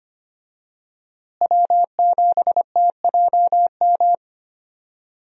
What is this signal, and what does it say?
Morse code, heard as a keyed tone, about 25 wpm, 700 Hz. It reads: W7TJM
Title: USA Call Signs - 5 Characters - Repeated 3x - 25wpm